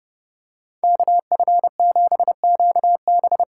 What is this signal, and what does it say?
Morse code, heard as a keyed tone, about 31 wpm, 700 Hz. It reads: KF7QB
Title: USA Call Signs - 6 Characters - 6x Character Spacing - 30wpm